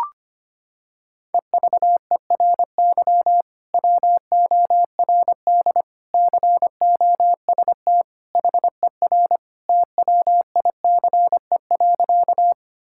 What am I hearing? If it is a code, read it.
Morse code, heard as a keyed tone, about 25 wpm, 700 Hz. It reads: EVERY WORD COST HER TWICE.
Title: Sentences from Top 1600 Words 25wpm